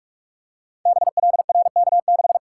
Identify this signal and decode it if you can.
Morse code, heard as a keyed tone, about 46 wpm, 700 Hz. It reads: BLRK6
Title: Code Groups - 5 Letter:Numbers - 45wpm